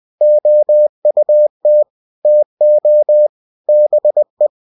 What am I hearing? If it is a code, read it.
Morse code, heard as a keyed tone, about 20 wpm, 600 Hz. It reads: OUT TO BE
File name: cw-books / winnie-the-pooh / winnie-20wpm-page-0022